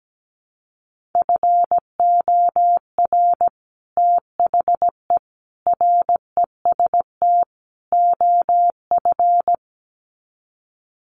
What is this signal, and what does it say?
Morse code, heard as a keyed tone, about 17 wpm, 700 Hz. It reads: FOR THE REST OF
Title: Sets of 4 Words 17wpm